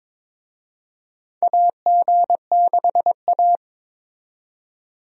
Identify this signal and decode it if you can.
Morse code, heard as a keyed tone, about 22 wpm, 700 Hz. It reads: AG6A
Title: USA Call Signs - 4 Characters - 22wpm